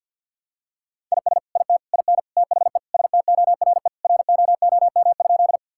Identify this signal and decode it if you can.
Morse code, heard as a keyed tone, about 50 wpm, 700 Hz. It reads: IS IT IN THE STORE ROOM?
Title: Sentences from Top 1000 Words - Review Part 5 - 50wpm